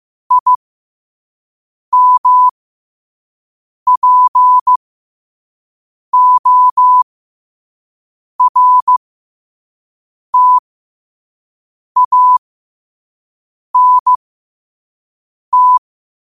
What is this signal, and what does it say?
Morse code, heard as a keyed tone, about 15 wpm, spaced as for 5 wpm, 1000 Hz. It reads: IMPORTANT
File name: qotdfiles / 5wpm